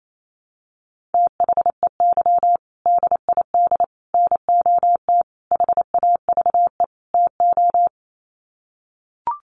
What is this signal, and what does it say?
Morse code, heard as a keyed tone, about 28 wpm, 700 Hz. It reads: THEY DID NOT HAVE TO
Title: Sets of 5 Words Encore2 28wpm